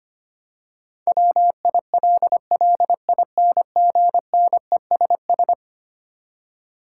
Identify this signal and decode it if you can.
Morse code, heard as a keyed tone, about 25 wpm, 700 Hz. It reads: WILLINGNESS
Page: morse-code-ninja-practice-25wpm